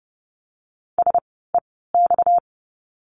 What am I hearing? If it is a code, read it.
Morse code, heard as a keyed tone, about 30 wpm, spaced as for 17 wpm, 700 Hz. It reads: SEX
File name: course / 30:10wpm / 230 Lesson 37f - Letter x - Words 30wpm